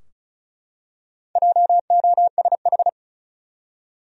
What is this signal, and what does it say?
Morse code, heard as a keyed tone, about 36 wpm, 700 Hz. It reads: JOSH
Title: USA Common Names - Max 4 Letters - 35wpm